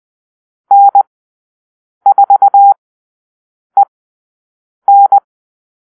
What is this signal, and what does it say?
Morse code, heard as a keyed tone, about 20 wpm, spaced as for 7 wpm, 800 Hz. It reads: N4EN
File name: arrl / Call-Sign Drill-8